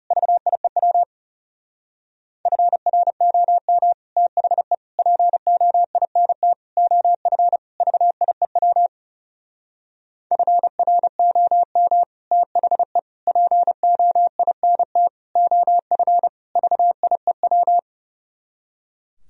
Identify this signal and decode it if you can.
Morse code, heard as a keyed tone, about 34 wpm, 700 Hz. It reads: VIEW FROM THE POINT OF VIEW FROM THE POINT OF VIEW
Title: Speed-Racing - Sets of 5 Words 30wpm